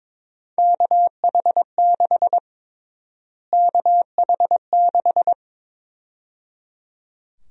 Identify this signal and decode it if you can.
Morse code, heard as a keyed tone, about 22 wpm, 700 Hz. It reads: KH6 KH6
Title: USA Call Sign Prefixes - Repeated 2x - 22wpm